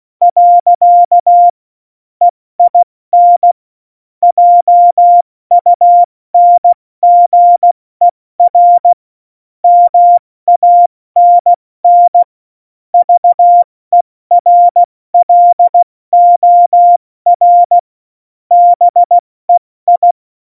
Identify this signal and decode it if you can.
Morse code, heard as a keyed tone, about 16 wpm, 700 Hz. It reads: . EIN JUNGER MANN VERLOR BEI
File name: Morsekurs / Witze / Witz 01. Dezember 2020, 70 F80